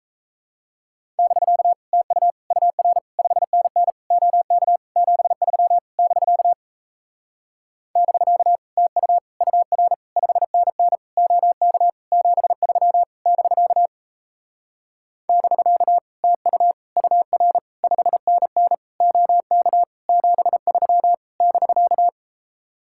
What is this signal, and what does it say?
Morse code, heard as a keyed tone, about 37 wpm, 700 Hz. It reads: <BK> TU UR 5NN OK 73 <BK> <BK> TU UR 5NN OK 73 <BK> <BK> TU UR 5NN OK 73 <BK>